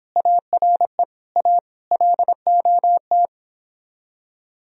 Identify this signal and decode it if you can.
Morse code, heard as a keyed tone, about 29 wpm, 700 Hz. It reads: ARE A LOT